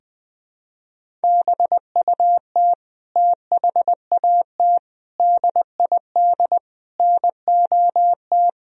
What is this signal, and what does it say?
Morse code, heard as a keyed tone, about 20 wpm, 700 Hz. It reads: BUT THAT DID NOT
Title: Sentences from Top 1000 Words - Review Part 1 - 20wpm